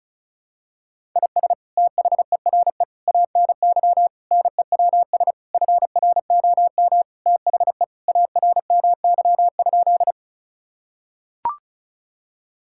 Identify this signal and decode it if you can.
Morse code, heard as a keyed tone, about 34 wpm, 700 Hz. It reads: IS THERE ANY NEWS FROM THE ARMY?
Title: Sentences from Top 1000 Words - Review Part 5 - 35wpm